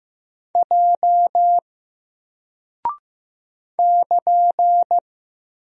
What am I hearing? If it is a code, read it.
Morse code, heard as a keyed tone, about 15 wpm, 700 Hz. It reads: J <KN>
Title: Single Character (Letter:Number:Punctuation:Prosign) 15wpm